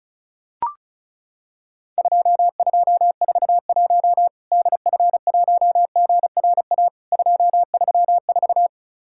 Morse code, heard as a keyed tone, about 34 wpm, 700 Hz. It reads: 2241 DF1GRA 234
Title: Worked All Europe DX Contest - QTC Exchange - 35wpm